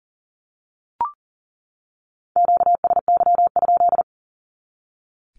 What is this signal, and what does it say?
Morse code, heard as a keyed tone, about 39 wpm, 700 Hz. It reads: QSY?
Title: Q-Codes 40wpm